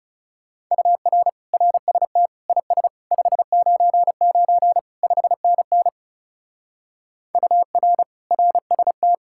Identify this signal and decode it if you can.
Morse code, heard as a keyed tone, about 37 wpm, 700 Hz. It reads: UR RST IS 599 5NN UR RST
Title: Speed-Racing Common QSO Phrases - 30wpm